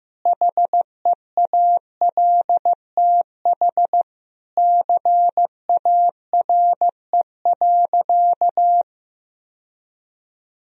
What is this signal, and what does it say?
Morse code, heard as a keyed tone, about 15 wpm, 700 Hz. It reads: HEALTH CARE.
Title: Sentences from Top 1000 Words - Review Part 4 - 15wpm